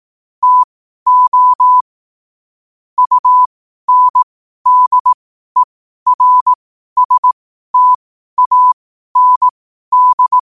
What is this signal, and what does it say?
Morse code, heard as a keyed tone, about 18 wpm, spaced as for 12 wpm, 1000 Hz. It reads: TO UNDERSTAND
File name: qotdfiles / 12wpm